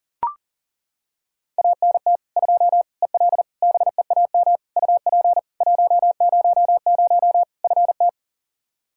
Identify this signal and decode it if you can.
Morse code, heard as a keyed tone, about 40 wpm, 700 Hz. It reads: ANT 2 EL BEAM UP 100 FT
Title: Practice QSO - Exchange 2 - Info - 40wpm